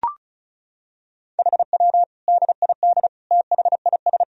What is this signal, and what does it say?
Morse code, heard as a keyed tone, about 36 wpm, 700 Hz. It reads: HW DID THIS
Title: Sentences from Top 400 Words and Abbreviations 35wpm